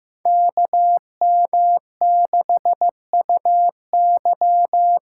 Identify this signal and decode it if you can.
Morse code, heard as a keyed tone, about 15 wpm, 700 Hz. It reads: KM6UY
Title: USA Call Signs - 8x Character Spacing - 15wpm